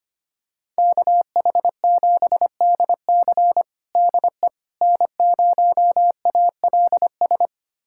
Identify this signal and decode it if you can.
Morse code, heard as a keyed tone, about 25 wpm, 700 Hz. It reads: KH7DC DE N0ALS
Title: USA Callsign DE Callsign - 5 Characters - 25wpm